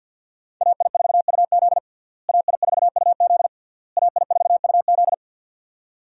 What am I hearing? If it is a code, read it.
Morse code, heard as a keyed tone, about 48 wpm, 700 Hz. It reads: AI4UZ AI4UZ AI4UZ
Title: USA Call Signs - 5 Characters - Repeated 3x - 50wpm